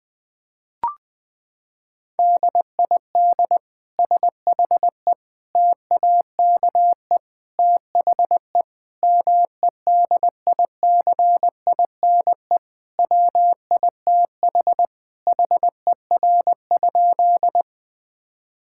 Morse code, heard as a keyed tone, about 20 wpm, 700 Hz. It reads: DID SHE TAKE THE MEDICINE WITH HER?
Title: Sentences from Top 1000 Words - Review Part 6 - 20wpm